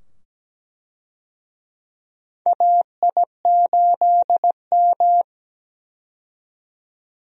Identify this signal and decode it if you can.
Morse code, heard as a keyed tone, about 17 wpm, 700 Hz. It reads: AI8M
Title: USA Call Signs - 4 Characters - 17wpm